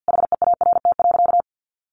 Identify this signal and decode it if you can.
Morse code, heard as a keyed tone, about 50 wpm, 700 Hz. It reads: HEART.
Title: Sentences from Top 1200 Words 50wpm